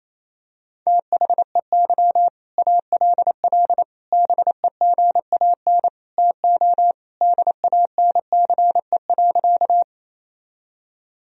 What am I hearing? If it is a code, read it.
Morse code, heard as a keyed tone, about 28 wpm, 700 Hz. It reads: THEY ALL BEGAN TO DANCE.
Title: Sentences from Top 1000 Words - Review Part 4 - 28wpm